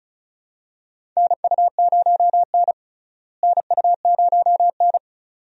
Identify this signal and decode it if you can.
Morse code, heard as a keyed tone, about 34 wpm, 700 Hz. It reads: NU0N NU0N
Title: USA Call Signs - 4 Characters - Repeated 2x - 35wpm